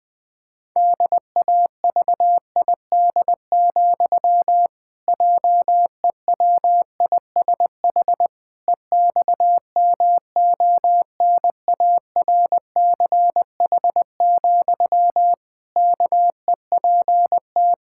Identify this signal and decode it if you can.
Morse code, heard as a keyed tone, about 21 wpm, 700 Hz. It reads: DAVID, JEWISH EXMONARCH, KEPT
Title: Pangram Challenge v2 - 20wpm